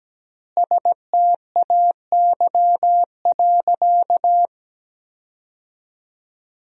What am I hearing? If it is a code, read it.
Morse code, heard as a keyed tone, about 17 wpm, 700 Hz. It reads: STAY.
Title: 4x Word-Spacing - Sentences from Top 500 Words - 17wpm